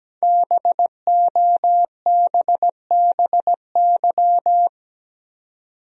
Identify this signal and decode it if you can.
Morse code, heard as a keyed tone, about 17 wpm, 700 Hz. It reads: BOBBY
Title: USA Common Names - 17wpm